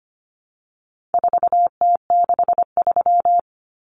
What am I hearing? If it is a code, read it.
Morse code, heard as a keyed tone, about 25 wpm, 700 Hz. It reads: 4T63
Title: Common Cut Numbers - 4-Digit 25wpm